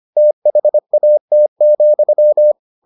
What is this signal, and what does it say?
Morse code, heard as a keyed tone, about 25 wpm, 600 Hz. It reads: THAT,